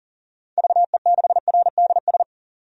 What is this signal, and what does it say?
Morse code, heard as a keyed tone, about 38 wpm, 700 Hz. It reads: VE6RDS
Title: DXCC-100 Call Signs - 6 Characters - 40wpm